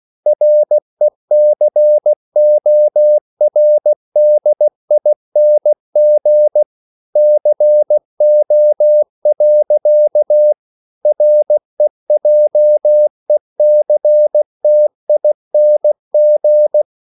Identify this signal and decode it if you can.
Morse code, heard as a keyed tone, about 16 wpm, 600 Hz. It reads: RECORDING CO. REJECTING
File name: CW / Practice / EnglishQuotations0416